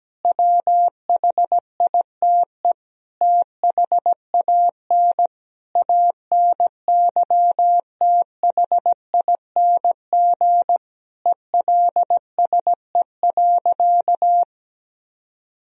Morse code, heard as a keyed tone, about 17 wpm, 700 Hz. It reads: WHITE THAN ANYTHING ELSE.
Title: Sentences from Top 1000 Words - Review Part 1 - 17wpm